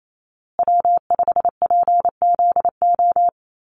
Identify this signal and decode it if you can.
Morse code, heard as a keyed tone, about 28 wpm, 700 Hz. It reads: W5PZO
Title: USA Call Signs Repeated 2x 28wpm